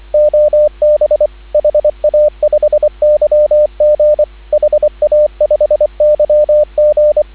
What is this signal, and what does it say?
Morse code, heard as a keyed tone, about 24 wpm, 590 Hz, with radio noise. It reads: OB HA5YG HA5YG
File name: CW / 2017 HA-OB HA5YG